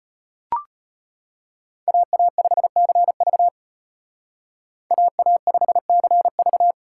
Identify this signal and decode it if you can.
Morse code, heard as a keyed tone, about 33 wpm, 700 Hz. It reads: AA5CV AA5CV